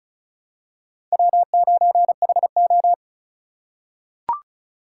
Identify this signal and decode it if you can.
Morse code, heard as a keyed tone, about 34 wpm, 700 Hz. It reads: W9HO
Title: USA Call Signs - 4x Character Spacing - 35wpm